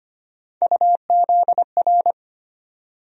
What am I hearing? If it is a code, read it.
Morse code, heard as a keyed tone, about 25 wpm, 700 Hz. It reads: UZR
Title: USA Call Sign Postfixes - 3 Letters 25wpm